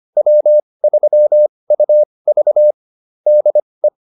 Morse code, heard as a keyed tone, about 25 wpm, spaced as for 20 wpm, 600 Hz. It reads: W3UV DE